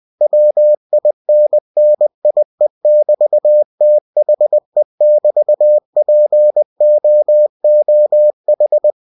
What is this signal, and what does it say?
Morse code, heard as a keyed tone, about 20 wpm, 600 Hz. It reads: WINNIE<BT>THE<BT>POOH
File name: cw-books / winnie-the-pooh / winnie-20wpm-page-0004